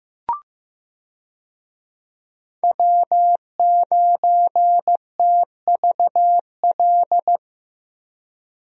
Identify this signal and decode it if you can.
Morse code, heard as a keyed tone, about 15 wpm, 700 Hz. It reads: W9TVL
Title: USA Call Signs - 5 Characters - Repeated 2x - 15wpm